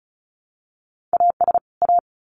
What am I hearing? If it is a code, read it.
Morse code, heard as a keyed tone, about 34 wpm, 700 Hz. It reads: AS A
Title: Sets of 2 Words 35wpm